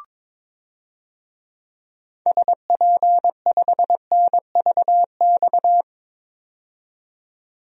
Morse code, heard as a keyed tone, about 22 wpm, 700 Hz. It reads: SP5NVX
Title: DXCC-100 Call Signs - 6 Characters - 22wpm